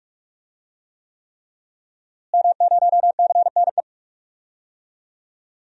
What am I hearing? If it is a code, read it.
Morse code, heard as a keyed tone, about 42 wpm, 700 Hz. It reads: M0CNE
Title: DXCC-100 Call Signs - 5 Characters - Repeated 3x - 45wpm